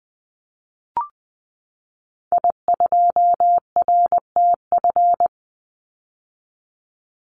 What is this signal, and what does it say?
Morse code, heard as a keyed tone, about 20 wpm, 700 Hz. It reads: I2RTF
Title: Call Signs - 20wpm